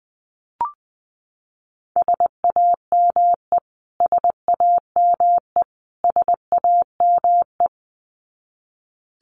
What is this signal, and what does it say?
Morse code, heard as a keyed tone, about 20 wpm, 700 Hz. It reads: SAME SAME SAME
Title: Top 500 Words Repeated 20wpm